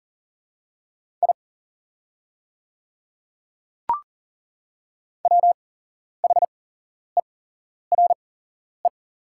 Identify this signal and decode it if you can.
Morse code, heard as a keyed tone, about 38 wpm, spaced as for 11 wpm, 700 Hz. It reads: I WHERE
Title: 8x Character Spacing - Top 200 Words - 40wpm